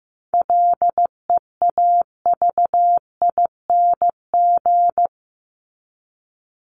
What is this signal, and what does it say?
Morse code, heard as a keyed tone, about 15 wpm, 700 Hz. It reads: LEAVING
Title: ING Suffix - Top 100 Words with Base Words 15wpm